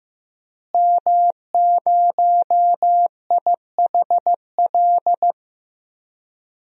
Morse code, heard as a keyed tone, about 15 wpm, 700 Hz. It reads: M0IHL
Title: DXCC-100 Call Signs - 5 Characters - Repeated 3x - 15wpm